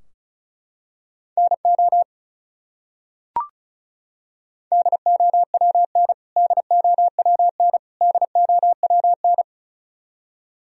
Morse code, heard as a keyed tone, about 36 wpm, 700 Hz. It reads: NO DOWN DOWN DOWN
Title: Top 100 Words Repeated 35wpm